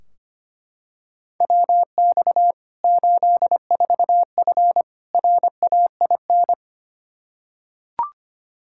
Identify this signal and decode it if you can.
Morse code, heard as a keyed tone, about 25 wpm, 700 Hz. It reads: WX 84F RAIN
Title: K1USN SST QSO Practice 25wpm